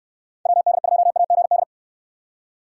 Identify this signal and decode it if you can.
Morse code, heard as a keyed tone, about 69 wpm, 700 Hz. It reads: WD2AGD